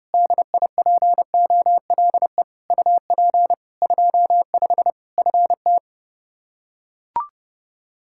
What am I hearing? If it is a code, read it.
Morse code, heard as a keyed tone, about 32 wpm, 700 Hz. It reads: DIPOLE UP 25 FT
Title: Practice QSO - Exchange 2 - Info - 30wpm